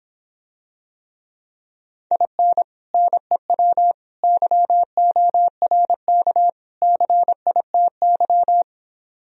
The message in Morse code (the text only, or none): IN NEW YORK CITY